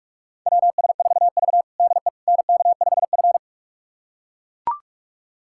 Morse code, heard as a keyed tone, about 41 wpm, 700 Hz. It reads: WS4V DE NK5F
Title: USA Callsign DE Callsign - 4 Characters - 45wpm